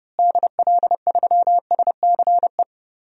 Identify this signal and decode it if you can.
Morse code, heard as a keyed tone, about 32 wpm, 700 Hz. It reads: DL3SCE
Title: DXCC-100 Call Signs - 6 Characters - Repeated 3x - 30wpm